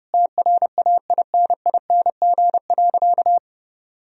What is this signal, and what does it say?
Morse code, heard as a keyed tone, about 30 wpm, 700 Hz. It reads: TRAINING.
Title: Sentences from Top 1000 Words - Review Part 4 - 30wpm